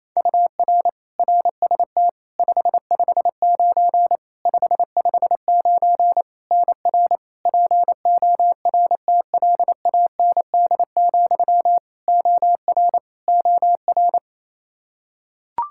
Morse code, heard as a keyed tone, about 28 wpm, 700 Hz. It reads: UR RST 559 559 NR PORTLAND, OR OR
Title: Practice QSO - Exchange 1 - Beginning - 28wpm